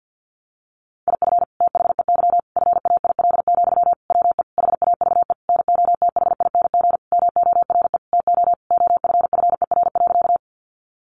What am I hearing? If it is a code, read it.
Morse code, heard as a keyed tone, about 50 wpm, 700 Hz. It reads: IF THEY FAIL, WE HAVE NOTHING MORE TO OFFER.
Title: Sentences from Top 1100 Words 50wpm